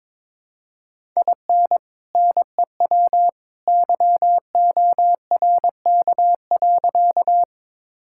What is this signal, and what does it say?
Morse code, heard as a keyed tone, about 21 wpm, 700 Hz. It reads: IN NEW YORK.
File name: practice / 22wpm / Sentences with 2-Word Phrases - 22wpm